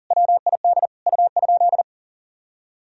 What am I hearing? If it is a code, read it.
Morse code, heard as a keyed tone, about 40 wpm, 700 Hz. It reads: WID U?
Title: Sentences from Top 100 Words and Abbreviations 40wpm